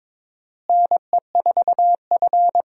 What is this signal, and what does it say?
Morse code, heard as a keyed tone, about 22 wpm, 700 Hz. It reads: NE4F
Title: USA Call Signs - 4 Characters - Repeated 3x - 22wpm